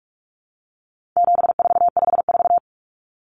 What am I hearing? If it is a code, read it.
Morse code, heard as a keyed tone, about 44 wpm, 700 Hz. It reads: 7454
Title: Numbers - 4-Digit 45wpm